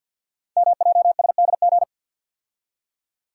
Morse code, heard as a keyed tone, about 50 wpm, 700 Hz. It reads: MJSDG
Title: Code Groups - 5 Letters - 50wpm